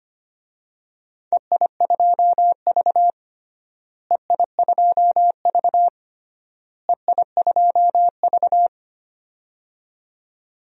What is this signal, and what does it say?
Morse code, heard as a keyed tone, about 25 wpm, 700 Hz. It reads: EI2V EI2V EI2V
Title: DXCC-100 Call Signs - 4 Characters - Repeated 3x - 25wpm